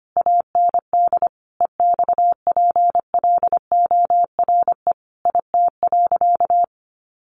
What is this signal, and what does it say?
Morse code, heard as a keyed tone, about 25 wpm, 700 Hz. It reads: AND EXPLORE IT.
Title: Sentences from Top 1100 Words 25wpm